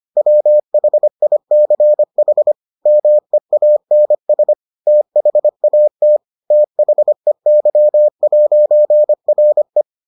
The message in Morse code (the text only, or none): WHICH MEANS THAT THEY'RE